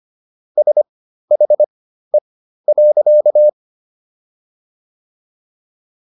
Morse code, heard as a keyed tone, about 25 wpm, spaced as for 13 wpm, 600 Hz. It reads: SHE.